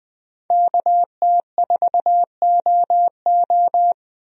Morse code, heard as a keyed tone, about 21 wpm, 700 Hz. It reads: KT4OO
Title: USA Call Signs - 5 Characters - 20wpm